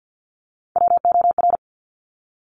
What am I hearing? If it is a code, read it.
Morse code, heard as a keyed tone, about 50 wpm, 700 Hz. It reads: WOR
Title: Top Three Letter Combinations in Words 50wpm